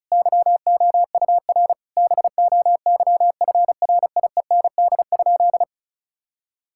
Morse code, heard as a keyed tone, about 34 wpm, 700 Hz. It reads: YOUR BOYFRIEND?
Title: Sentences from Top 1000 Words - Review Part 5 - 35wpm